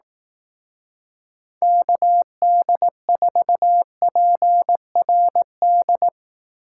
Morse code, heard as a keyed tone, about 19 wpm, 700 Hz. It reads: KD4PRD
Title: Speed Racing - USA 2x3 Call Signs 15wpm